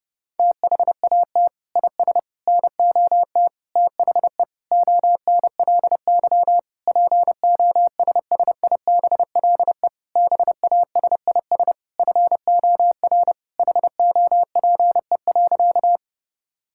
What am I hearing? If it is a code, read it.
Morse code, heard as a keyed tone, about 31 wpm, 700 Hz. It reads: THAT IS NOT THE ONLY POSSIBLE BASIS FOR HOPE.
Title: Sentences from Top 1000 Words - Review Part 5 - 30wpm